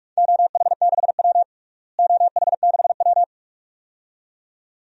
Morse code, heard as a keyed tone, about 46 wpm, 700 Hz. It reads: OH6W OH6W
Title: DXCC-100 Call Signs - 4 Characters - Repeated 2x - 45wpm